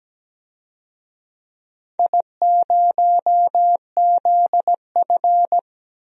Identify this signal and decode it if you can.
Morse code, heard as a keyed tone, about 17 wpm, 700 Hz. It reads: I0ZF
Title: DXCC-100 Call Signs - 4 Characters - Repeated 2x - 17wpm